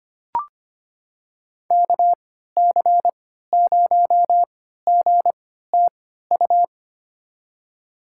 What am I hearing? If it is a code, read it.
Morse code, heard as a keyed tone, about 25 wpm, spaced as for 14 wpm, 700 Hz. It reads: KC0GTU
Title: USA Call Signs - 6 Characters - 3x Character Spacing - 25wpm